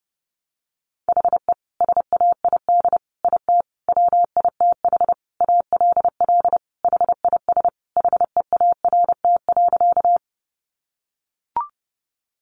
Morse code, heard as a keyed tone, about 30 wpm, 700 Hz. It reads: HE SAID IT WITH ALL HIS HEART.
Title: Sentences from Top 1200 Words 30wpm